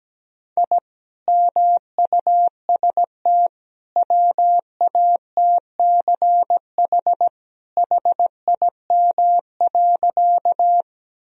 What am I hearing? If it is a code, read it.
Morse code, heard as a keyed tone, about 17 wpm, 700 Hz. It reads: I MUST WATCH HIM.
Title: Sentences from Top 1000 Words - Review Part 1 - 17wpm